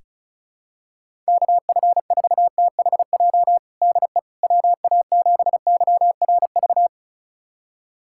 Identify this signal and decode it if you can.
Morse code, heard as a keyed tone, about 36 wpm, 700 Hz. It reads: KF4THJ DE WA7YRV
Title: USA Callsign DE Callsign 35wpm